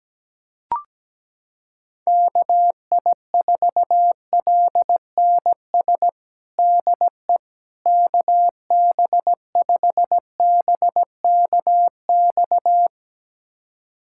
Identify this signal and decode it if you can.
Morse code, heard as a keyed tone, about 17 wpm, 700 Hz. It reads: KI4LNS DE KB5BKX
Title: USA Callsign DE Callsign 17wpm